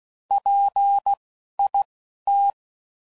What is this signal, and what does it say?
Morse code, heard as a keyed tone, about 16 wpm, spaced as for 12 wpm, 800 Hz. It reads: PIT
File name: Prefixes / By First Letter / Ps